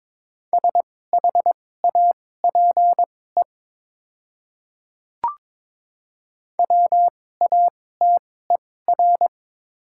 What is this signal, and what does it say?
Morse code, heard as a keyed tone, about 21 wpm, spaced as for 16 wpm, 700 Hz. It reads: SHAPE WATER
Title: 2x Character Spacing - Top 500 Words - 22wpm